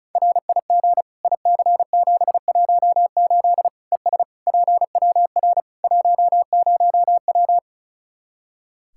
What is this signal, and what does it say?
Morse code, heard as a keyed tone, about 36 wpm, 700 Hz. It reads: RIG IC718 ES PWR 10W
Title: Practice QSO - Exchange 2 - Info - 35wpm